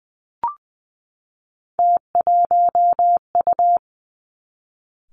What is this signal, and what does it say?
Morse code, heard as a keyed tone, about 20 wpm, 700 Hz. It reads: T1U
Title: Three Challenging Letter-Number Combinations 20wpm